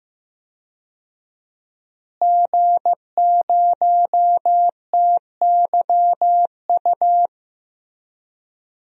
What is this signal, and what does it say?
Morse code, heard as a keyed tone, about 15 wpm, 700 Hz. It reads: G0TYU
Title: DXCC-100 Call Signs - 5 Characters - Repeated 2x - 15wpm